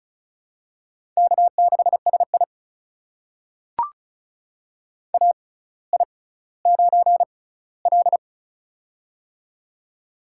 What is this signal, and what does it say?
Morse code, heard as a keyed tone, about 36 wpm, spaced as for 18 wpm, 700 Hz. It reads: K6SI AI9L
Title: USA Call Signs - 4 Characters - 6x Character Spacing - 35wpm